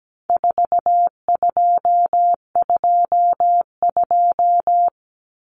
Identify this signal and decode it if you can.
Morse code, heard as a keyed tone, about 17 wpm, 700 Hz. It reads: 4222
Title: Common Cut Numbers - 4-Digit 17wpm